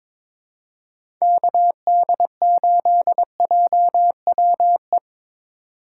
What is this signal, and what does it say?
Morse code, heard as a keyed tone, about 22 wpm, 700 Hz. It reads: KD8JWE
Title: USA Call Signs - 4x Character Spacing - 22wpm